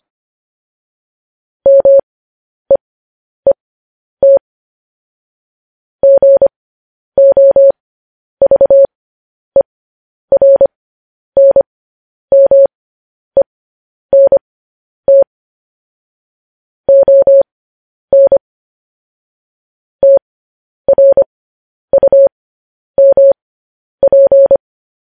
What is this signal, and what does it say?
Morse code, heard as a keyed tone, about 25 wpm, spaced as for 10 wpm, 550 Hz. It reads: MEET GOVERNMENT ON TRUMP
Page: news-headlines-in-morse-code-at-10-wpm